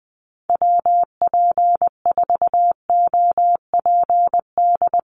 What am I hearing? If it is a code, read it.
Morse code, heard as a keyed tone, about 20 wpm, 700 Hz. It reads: WP4OPD
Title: USA Call Signs Repeated 2x 20wpm